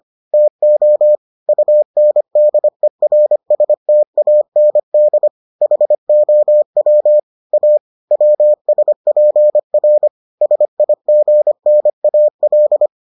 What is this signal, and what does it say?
Morse code, heard as a keyed tone, about 25 wpm, 600 Hz. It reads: TO UNDERSTAND HOW A WSPR SIGNAL